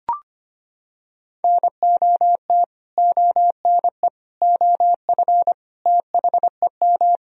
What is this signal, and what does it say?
Morse code, heard as a keyed tone, about 25 wpm, 700 Hz. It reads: NOT ONE OF THEM
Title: Sentences from Top 1000 Words - Review Part 4 - 25wpm